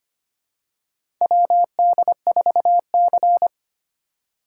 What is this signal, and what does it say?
Morse code, heard as a keyed tone, about 25 wpm, 700 Hz. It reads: WD4C
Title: USA Call Signs - 4 Characters - 2x Character Spacing - 25wpm